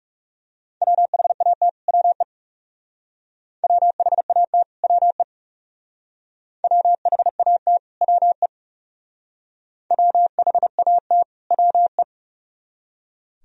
Morse code, heard as a keyed tone, about 41 wpm, 700 Hz. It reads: WHAT WE WHAT WE WHAT WE WHAT WE